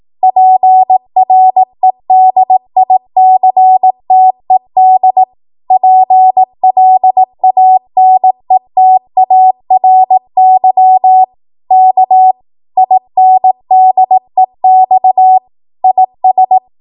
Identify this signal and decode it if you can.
Morse code, heard as a keyed tone, about 18 wpm, 750 Hz. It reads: PREDICTED PLANETARY K INDEX IS
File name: Morse / 18WPM